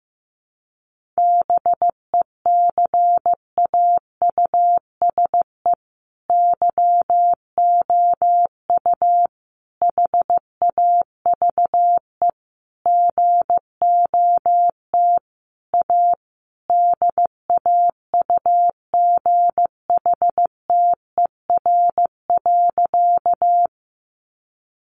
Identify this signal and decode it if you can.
Morse code, heard as a keyed tone, about 15 wpm, 700 Hz. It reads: BECAUSE YOU HAVE GOT A DAUGHTER.
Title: Sentences from Top 1500 Words 15wpm